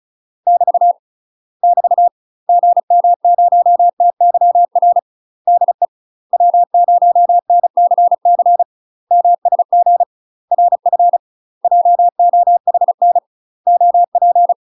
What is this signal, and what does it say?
Morse code, heard as a keyed tone, about 36 wpm, 700 Hz. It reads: <BT> <BT> GM0TYR DE W0NCC MSG RF JOHN OP
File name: UK MASTER QSOs TRAINING / 35WPM / UK MASTER QSOs 0014